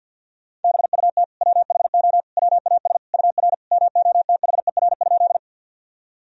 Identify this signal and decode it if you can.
Morse code, heard as a keyed tone, about 47 wpm, 700 Hz. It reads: BUT WHO WAS UR MOTHER?
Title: Sentences from Top 200 Words and Abbreviations 50wpm